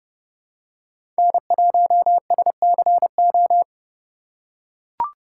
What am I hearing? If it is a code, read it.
Morse code, heard as a keyed tone, about 31 wpm, 700 Hz. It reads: N1SCO